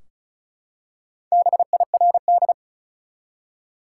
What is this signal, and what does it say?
Morse code, heard as a keyed tone, about 34 wpm, 700 Hz. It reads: BIRD
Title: Four Letter Words - 35wpm